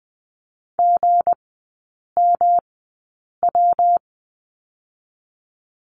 Morse code, heard as a keyed tone, about 20 wpm, spaced as for 8 wpm, 700 Hz. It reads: GMW
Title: 184 Lesson 30e - Letter b - Three Characters 20wpm